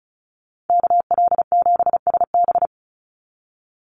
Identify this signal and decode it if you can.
Morse code, heard as a keyed tone, about 34 wpm, 700 Hz. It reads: KL7SB
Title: Call Signs - 35wpm